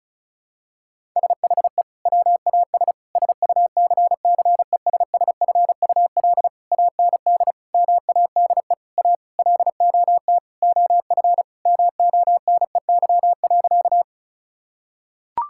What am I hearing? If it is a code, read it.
Morse code, heard as a keyed tone, about 36 wpm, 700 Hz. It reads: SHE WAS SUCCESSFUL AND MADE A LOT OF MONEY.
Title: Sentences from Top 1000 Words - Review Part 6 - 35wpm